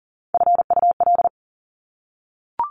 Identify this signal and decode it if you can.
Morse code, heard as a keyed tone, about 42 wpm, 700 Hz. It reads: FUL
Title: Top Three Letter Combinations in Words 40wpm